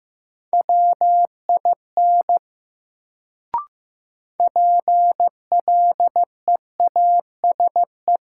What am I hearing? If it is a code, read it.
Morse code, heard as a keyed tone, about 15 wpm, 700 Hz. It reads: WIN PLEASE
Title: Sentences from Top 1500 Words - New Word Drill - 15wpm